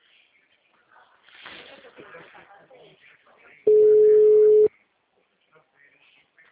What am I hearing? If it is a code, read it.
Ringback tone (Europe, 425 Hz), with speech.